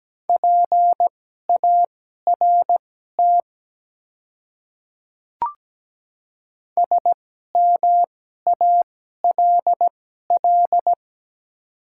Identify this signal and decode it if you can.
Morse code, heard as a keyed tone, about 17 wpm, spaced as for 12 wpm, 700 Hz. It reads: PART SMALL
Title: 2x Character Spacing - Top 200 Words - 17wpm